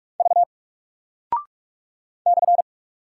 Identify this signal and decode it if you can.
Morse code, heard as a keyed tone, about 44 wpm, 700 Hz. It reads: V /
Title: Single Character (Letter:Number:Punctuation:Prosign) 45wpm